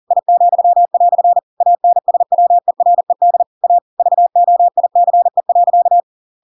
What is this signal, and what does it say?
Morse code, heard as a keyed tone, about 40 wpm, 700 Hz. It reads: I," ANSWERED A VOICE.
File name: Pinocchio / Pinocchio ch6 40 40 ik1hge